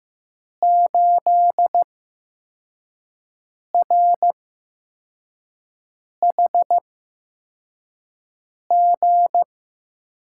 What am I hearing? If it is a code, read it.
Morse code, heard as a keyed tone, about 15 wpm, spaced as for 4 wpm, 700 Hz. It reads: 8RHG
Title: USA Call Signs - 5 Characters - 8x Character Spacing - 15wpm